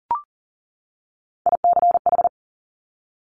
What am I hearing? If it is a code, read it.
Morse code, heard as a keyed tone, about 40 wpm, 700 Hz. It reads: ICH
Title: Top Three Letter Combinations in Words 40wpm